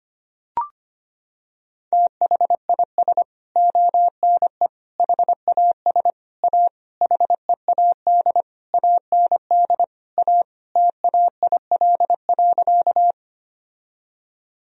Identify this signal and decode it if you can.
Morse code, heard as a keyed tone, about 25 wpm, 700 Hz. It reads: THIS ONE HAS A HEAD AND A TAIL.
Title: Sentences from Top 1000 Words - Review Part 3 - 25wpm